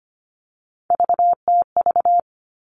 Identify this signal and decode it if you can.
Morse code, heard as a keyed tone, about 25 wpm, 700 Hz. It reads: VTV
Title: All Cut Numbers - 3-Digit 25wpm